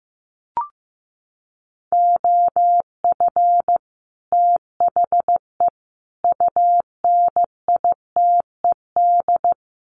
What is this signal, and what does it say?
Morse code, heard as a keyed tone, about 15 wpm, 700 Hz. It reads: OF THE UNITED
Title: Sets of 3 Words 15wpm